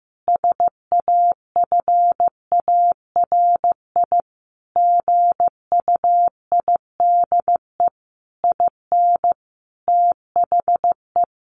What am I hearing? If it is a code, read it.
Morse code, heard as a keyed tone, about 15 wpm, 700 Hz. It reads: SAFARI GUIDE IN THE
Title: A Sound Of Thunder 15wpm